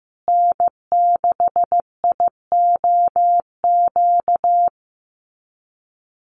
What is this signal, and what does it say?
Morse code, heard as a keyed tone, about 15 wpm, 700 Hz. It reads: N6IOQ
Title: USA Call Signs Repeated 3x 15wpm